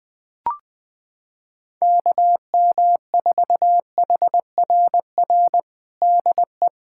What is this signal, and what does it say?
Morse code, heard as a keyed tone, about 21 wpm, 700 Hz. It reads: KM4HRR DE
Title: USA Callsign DE Callsign 20wpm